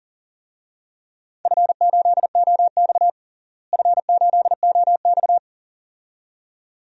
Morse code, heard as a keyed tone, about 40 wpm, 700 Hz. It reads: F8OX F8OX
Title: DXCC-100 Call Signs - 4 Characters - Repeated 2x - 40wpm